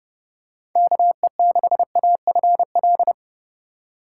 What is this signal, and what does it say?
Morse code, heard as a keyed tone, about 31 wpm, 700 Hz. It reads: KE6AFL
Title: USA Call Signs - 6 Characters - 4x Character Spacing - 30wpm